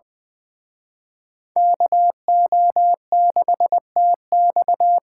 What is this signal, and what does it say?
Morse code, heard as a keyed tone, about 23 wpm, 700 Hz. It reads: KO6TX
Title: Speed Racing - USA 2x2 Call Signs 20wpm